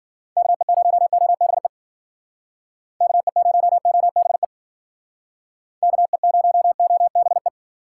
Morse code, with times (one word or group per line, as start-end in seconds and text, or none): 0.4-1.7 KE0OBE
3.0-4.5 KE0OBE
5.8-7.5 KE0OBE